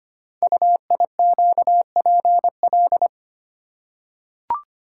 Morse code, heard as a keyed tone, about 25 wpm, 700 Hz. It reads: UIQPL
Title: Code Groups - 5 Letters - 25wpm